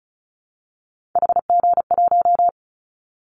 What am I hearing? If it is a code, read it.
Morse code, heard as a keyed tone, about 34 wpm, 700 Hz. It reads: HG1
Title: Callsign Prefixes for Top 25 DXCC - Repeated 3x - 35wpm